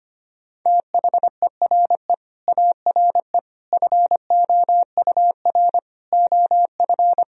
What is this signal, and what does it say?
Morse code, heard as a keyed tone, about 25 wpm, 700 Hz. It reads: THERE ARE FOUR OF
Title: Sentences from Top 1000 Words - Review Part 1 - 25wpm